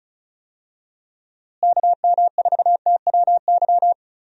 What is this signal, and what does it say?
Morse code, heard as a keyed tone, about 36 wpm, 700 Hz. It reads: KM4TWY
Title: USA Call Signs - 6 Characters - Repeated 2x - 35wpm